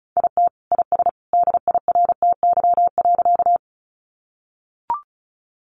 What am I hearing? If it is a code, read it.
Morse code, heard as a keyed tone, about 34 wpm, 700 Hz. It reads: IT IS DIRTY.
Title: Additional Easy Phrases 35wpm